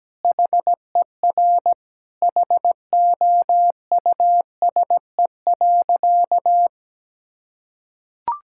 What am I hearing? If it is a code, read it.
Morse code, heard as a keyed tone, about 17 wpm, 700 Hz. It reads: HER HOUSE.
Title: Sentences from Top 1000 Words - Review Part 6 - 17wpm